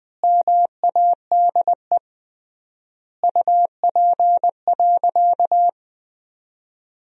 Morse code, heard as a keyed tone, about 21 wpm, 700 Hz. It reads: MADE UP.
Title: 3x Word-Spacing - Sentences from Top 500 Words - 20wpm